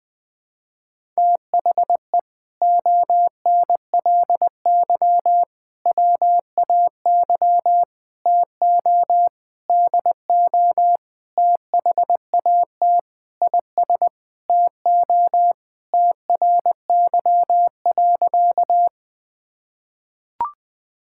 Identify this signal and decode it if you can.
Morse code, heard as a keyed tone, about 19 wpm, 700 Hz. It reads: THE ONLY WAY TO DO THAT IS TO TRY.
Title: Sentences with 8-Word Phrases - 20wpm